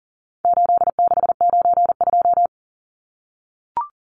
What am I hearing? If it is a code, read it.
Morse code, heard as a keyed tone, about 40 wpm, 700 Hz. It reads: 8692
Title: Numbers - 4-Digit 40wpm